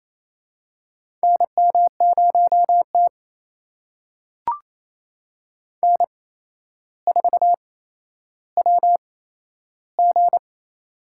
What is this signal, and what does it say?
Morse code, heard as a keyed tone, about 27 wpm, spaced as for 12 wpm, 700 Hz. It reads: NM0T N4WG
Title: USA Call Signs - 4 Characters - 8x Character Spacing - 28wpm